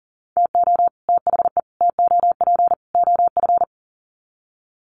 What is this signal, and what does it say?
Morse code, heard as a keyed tone, about 40 wpm, 700 Hz. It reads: TO THE TOP OF
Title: Sets of 4 Words 40wpm